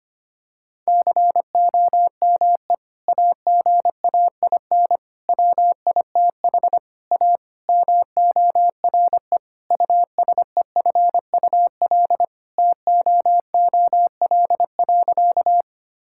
Morse code, heard as a keyed tone, about 25 wpm, 700 Hz. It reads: COME AGAIN WITH A MORE USEFUL TOOL.
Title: Sentences from Top 1000 Words - Review Part 6 - 25wpm